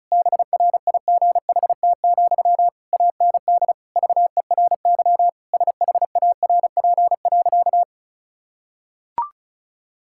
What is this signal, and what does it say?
Morse code, heard as a keyed tone, about 34 wpm, 700 Hz. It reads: BRIGHT, AND VERY SHARP.
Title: Sentences from Top 1600 Words 35wpm